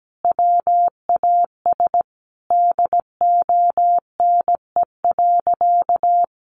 Morse code, heard as a keyed tone, about 17 wpm, 700 Hz. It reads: WAS DONE.
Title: Sentences from Top 1400 Words 17wpm